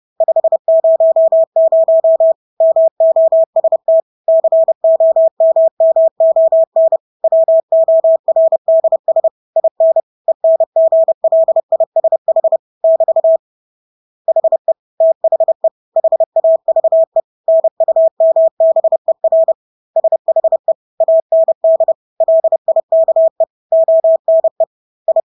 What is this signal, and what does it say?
Morse code, heard as a keyed tone, about 30 wpm, 650 Hz. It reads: 500 MOST COMMON WORDS IN ENGLISH <BT> HE THE HAVE NUMBER SHE AND LIKE ONE I
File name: Engelske ord 30-30 wpm